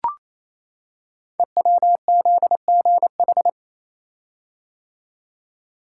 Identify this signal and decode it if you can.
Morse code, heard as a keyed tone, about 27 wpm, 700 Hz. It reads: EWZGH
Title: Code Groups - 5 Letters - 28wpm